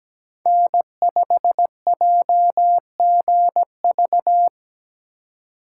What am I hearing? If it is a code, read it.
Morse code, heard as a keyed tone, about 17 wpm, 700 Hz. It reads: N5JGV
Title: USA Call Signs - 3x Character Spacing - 17wpm